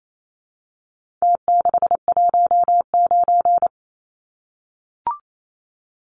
Morse code, heard as a keyed tone, about 28 wpm, 700 Hz. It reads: T619
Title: Numbers - Contest Style - 4-Digit 28wpm